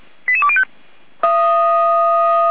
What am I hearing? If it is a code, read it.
Five-tone selective call (ZVEI), heard as quick runs of five tones, tones of about 70 ms. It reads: 88205